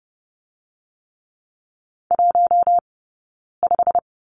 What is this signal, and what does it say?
Morse code, heard as a keyed tone, about 30 wpm, spaced as for 9 wpm, 690 Hz. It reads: 15
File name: course / 30:10wpm / 086 Lesson 15d - Number 5 - Two Characters 30wpm